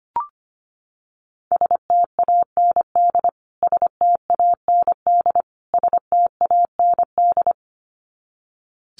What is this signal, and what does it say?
Morse code, heard as a keyed tone, about 25 wpm, 700 Hz. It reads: STAND STAND STAND
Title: Top 300 Words Repeated 25wpm